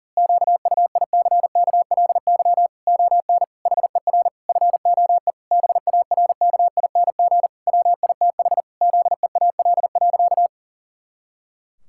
Morse code, heard as a keyed tone, about 40 wpm, 700 Hz. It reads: QUICKLY ON HER FOE BARKING WITH ZEAL.
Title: Pangram Challenge v2 - 40wpm